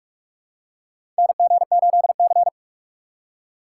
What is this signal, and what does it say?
Morse code, heard as a keyed tone, about 43 wpm, 700 Hz. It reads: NG8C
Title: USA Call Signs - 4 Characters - 4x Character Spacing - 45wpm